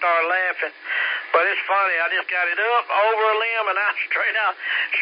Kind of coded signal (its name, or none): Emergency Alert System (SAME) tones